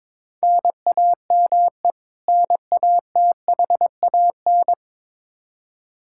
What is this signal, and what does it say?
Morse code, heard as a keyed tone, about 22 wpm, 700 Hz. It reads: NAME NATHAN
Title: Practice QSO - Exchange 1 - Beginning - 22wpm